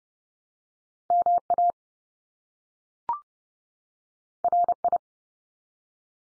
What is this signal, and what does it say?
Morse code, heard as a keyed tone, about 30 wpm, 700 Hz. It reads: MA RI
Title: US States 30WPM